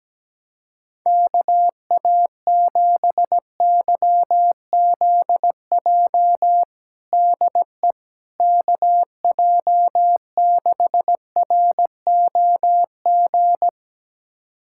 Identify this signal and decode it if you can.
Morse code, heard as a keyed tone, about 17 wpm, 700 Hz. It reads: KA7YZJ DE KJ6ROG
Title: USA Callsign DE Callsign - 6 Characters - 17wpm